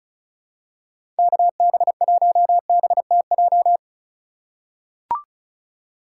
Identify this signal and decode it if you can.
Morse code, heard as a keyed tone, about 36 wpm, 700 Hz. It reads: KB1BTJ
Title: USA Call Signs - 6 Characters - 3x Character Spacing - 35wpm